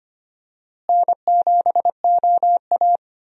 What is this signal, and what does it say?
Morse code, heard as a keyed tone, about 25 wpm, 700 Hz. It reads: N7OA
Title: USA Call Signs - 4 Characters - Repeated 3x - 25wpm